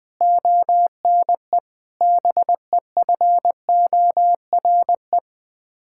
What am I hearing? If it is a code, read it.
Morse code, heard as a keyed tone, about 20 wpm, 700 Hz. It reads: ONE BEFORE
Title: Sets of 8 Words - 20wpm